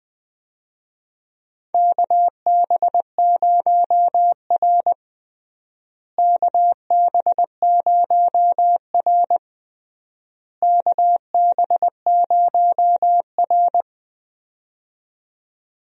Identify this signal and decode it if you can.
Morse code, heard as a keyed tone, about 20 wpm, 700 Hz. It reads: KB0R KB0R KB0R
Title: USA Call Signs - 4 Characters - Repeated 3x - 20wpm